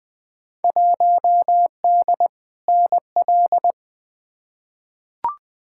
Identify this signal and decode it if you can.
Morse code, heard as a keyed tone, about 20 wpm, 700 Hz. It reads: 1D NL
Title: Field Day Exchanges Encore - 20wpm